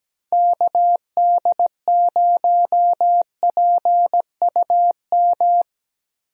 Morse code, heard as a keyed tone, about 17 wpm, 700 Hz. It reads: KD0PUM
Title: USA Call Signs - 6 Characters - Repeated 3x - 17wpm